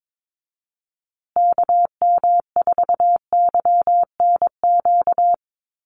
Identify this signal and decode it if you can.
Morse code, heard as a keyed tone, about 22 wpm, 700 Hz. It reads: KM4YNQ
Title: USA Call Signs Repeated 2x 22wpm